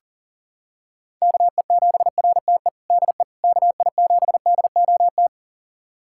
Morse code, heard as a keyed tone, about 40 wpm, 700 Hz. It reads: KE7RTE DE KI7DOT
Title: USA Callsign DE Callsign - 6 Characters - 40wpm